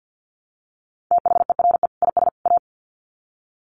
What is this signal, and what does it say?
Morse code, heard as a keyed tone, about 50 wpm, 700 Hz. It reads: THERE IS A
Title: Sets of 3 Words 50wpm